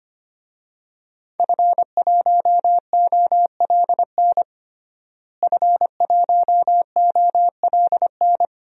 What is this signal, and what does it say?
Morse code, heard as a keyed tone, about 25 wpm, 700 Hz. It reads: F1OLN F1OLN
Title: DXCC-100 Call Signs - 5 Characters - Repeated 3x - 25wpm